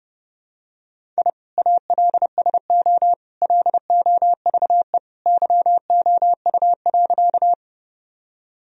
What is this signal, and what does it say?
Morse code, heard as a keyed tone, about 30 wpm, 700 Hz. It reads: I ALSO LOVE YOU.
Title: Sentences from Top 1000 Words - Review Part 1 - 30wpm